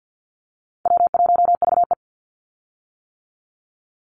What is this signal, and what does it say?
Morse code, heard as a keyed tone, about 50 wpm, 700 Hz. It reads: W1VE
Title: Call Signs - 50wpm